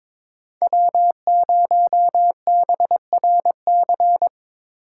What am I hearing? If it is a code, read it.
Morse code, heard as a keyed tone, about 26 wpm, 700 Hz. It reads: W0BRC
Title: Speed Racing - USA 1x3 Call Signs 22wpm